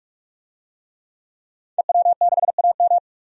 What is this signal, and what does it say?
Morse code, heard as a keyed tone, about 46 wpm, 700 Hz. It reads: EW6AM
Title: DXCC-100 Call Signs - 5 Characters - 45wpm